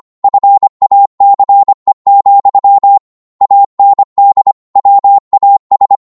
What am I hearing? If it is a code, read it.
Morse code, heard as a keyed tone, about 25 wpm, 800 Hz. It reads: FACE, AND WAS
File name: Morse / psg-25wpm